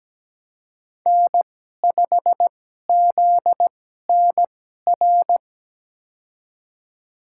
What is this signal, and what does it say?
Morse code, heard as a keyed tone, about 17 wpm, spaced as for 12 wpm, 700 Hz. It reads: N5ZNR
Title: USA Call Signs - 5 Characters - 2x Character Spacing - 17wpm